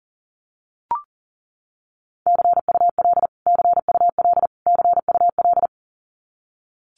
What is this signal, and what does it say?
Morse code, heard as a keyed tone, about 40 wpm, 700 Hz. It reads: CUL CUL CUL
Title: Abbreviations Repeated 3x - 40wpm